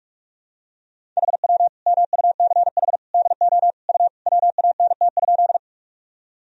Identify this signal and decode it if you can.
Morse code, heard as a keyed tone, about 42 wpm, 700 Hz. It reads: HW MUCH DO U WANT?
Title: Sentences from Top 200 Words and Abbreviations 45wpm